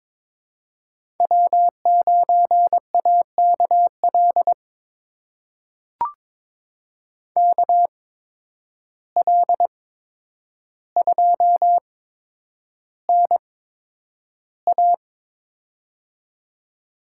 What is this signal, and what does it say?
Morse code, heard as a keyed tone, about 22 wpm, spaced as for 9 wpm, 700 Hz. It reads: W9AKL KL2NA
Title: USA Call Signs - 5 Characters - 8x Character Spacing - 22wpm